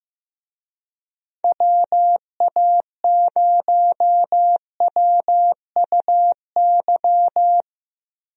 Morse code, heard as a keyed tone, about 15 wpm, 700 Hz. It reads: WA0WUY
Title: USA Call Signs - 6 Characters - Repeated 3x - 15wpm